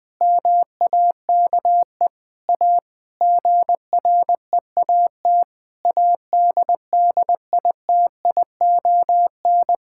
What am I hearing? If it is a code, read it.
Morse code, heard as a keyed tone, about 21 wpm, 700 Hz. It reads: MAKE A GREAT ADDITION
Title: Sentences from Top 1000 Words - Review Part 6 - 20wpm